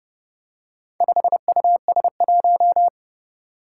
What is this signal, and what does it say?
Morse code, heard as a keyed tone, about 30 wpm, 700 Hz. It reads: 5US1
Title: Four Challenging Letter-Number Combinations 30wpm